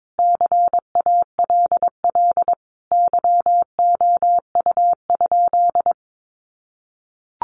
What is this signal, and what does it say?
Morse code, heard as a keyed tone, about 22 wpm, 700 Hz. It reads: CALL YOU?
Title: Sentences from Top 100 Words 22wpm